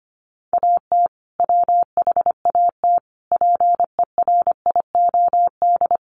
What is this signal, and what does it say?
Morse code, heard as a keyed tone, about 25 wpm, 700 Hz. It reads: AT WHAT PERIOD
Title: Sentences from Top 800 Words 25wpm